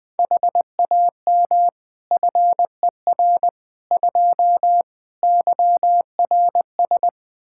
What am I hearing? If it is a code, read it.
Morse code, heard as a keyed tone, about 20 wpm, 700 Hz. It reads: HAM FER 2 YRS
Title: Practice QSO - Exchange 3 - More Info - 20wpm